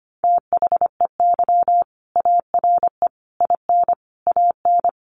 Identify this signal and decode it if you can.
Morse code, heard as a keyed tone, about 25 wpm, 700 Hz. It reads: THEY ARE IN AN
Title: Sentences from Top 1000 Words 25wpm